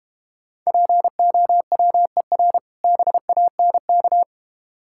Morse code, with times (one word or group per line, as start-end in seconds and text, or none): 0.7-2.6 POWER
2.8-4.2 BANK